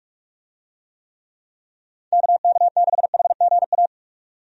Morse code, heard as a keyed tone, about 46 wpm, 700 Hz. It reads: KK6HGA
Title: USA Call Signs - 6 Characters - 45wpm